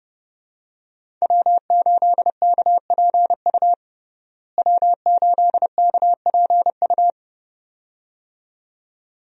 Morse code, heard as a keyed tone, about 29 wpm, 700 Hz. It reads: W8KPU W8KPU
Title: USA Call Signs - 5 Characters - Repeated 2x - 30wpm